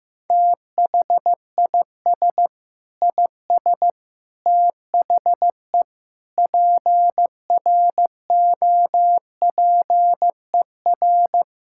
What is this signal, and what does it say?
Morse code, heard as a keyed tone, about 16 wpm, 700 Hz. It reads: THIS IS THE PROPER
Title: Sentences from Top 1000 Words - Review Part 5 - 15wpm